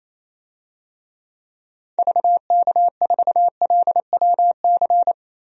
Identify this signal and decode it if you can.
Morse code, heard as a keyed tone, about 28 wpm, 700 Hz. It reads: VK4LWC
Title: DXCC-100 Call Signs - 6 Characters - Repeated 3x - 28wpm